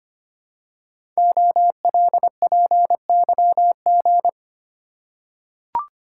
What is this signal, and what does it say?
Morse code, heard as a keyed tone, about 25 wpm, 700 Hz. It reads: OLPYG
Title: Code Groups - 5 Letters - 25wpm